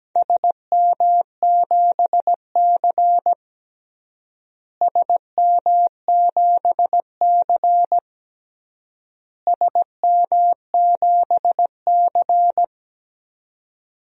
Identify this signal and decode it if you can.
Morse code, heard as a keyed tone, about 17 wpm, 700 Hz. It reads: SM7C SM7C SM7C
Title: DXCC-100 Call Signs - 4 Characters - Repeated 3x - 17wpm